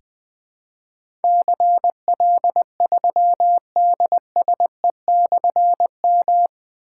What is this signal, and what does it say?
Morse code, heard as a keyed tone, about 20 wpm, 700 Hz. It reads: CL3DSE/M
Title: Difficult Call Signs 20wpm